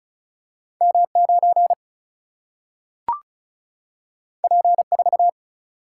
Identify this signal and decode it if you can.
Morse code, heard as a keyed tone, about 36 wpm, 700 Hz. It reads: M 9 P 4
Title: Single-Letter Single-Number 35wpm